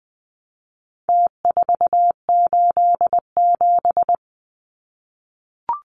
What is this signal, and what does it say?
Morse code, heard as a keyed tone, about 20 wpm, 700 Hz. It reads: T487
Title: Numbers - Contest Style - 4-Digit 20wpm